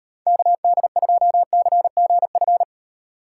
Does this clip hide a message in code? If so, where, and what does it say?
Yes: Morse code from 0.3 to 2.6 s, heard as a keyed tone, about 38 wpm, 700 Hz. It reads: KD2CGF